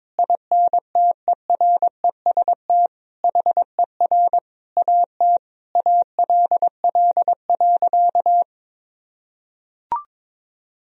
Morse code, heard as a keyed tone, about 22 wpm, 700 Hz. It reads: INTEREST HER AT ALL.
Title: Sentences from Top 1000 Words - Review Part 2 - 22wpm